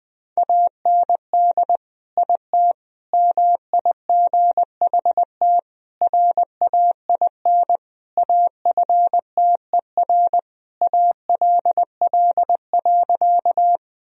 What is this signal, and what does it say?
Morse code, heard as a keyed tone, about 21 wpm, 700 Hz. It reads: AND IT MIGHT RAIN AFTER ALL.
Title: Sentences from Top 1000 Words - Review Part 1 - 20wpm